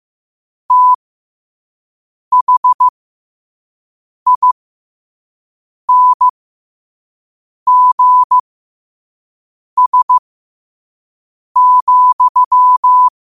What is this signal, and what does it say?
Morse code, heard as a keyed tone, about 15 wpm, spaced as for 5 wpm, 1000 Hz. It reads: THINGS,
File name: qotdfiles / 5wpm